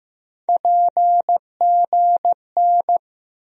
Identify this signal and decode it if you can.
Morse code, heard as a keyed tone, about 15 wpm, 700 Hz. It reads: PGN
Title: USA Call Sign Postfixes - 3 Letters 15wpm